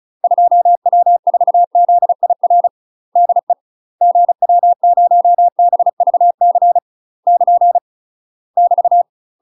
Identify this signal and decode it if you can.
Morse code, heard as a keyed tone, about 36 wpm, 700 Hz. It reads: 2W4ZIR DE GW0BVC <KN> <BT>
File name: UK MASTER QSOs TRAINING / 35WPM / UK MASTER QSOs 0009